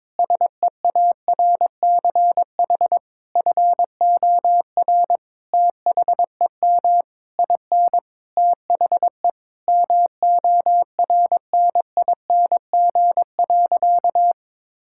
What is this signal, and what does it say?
Morse code, heard as a keyed tone, about 22 wpm, 700 Hz. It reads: SEARCH FOR THEM IN THE MORNING.
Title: Sentences from Top 1000 Words - Review Part 5 - 22wpm